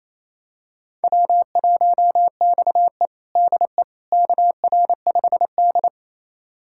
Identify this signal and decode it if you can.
Morse code, heard as a keyed tone, about 28 wpm, 700 Hz. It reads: W1XE DE KR5D
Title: USA Callsign DE Callsign - 4 Characters - 28wpm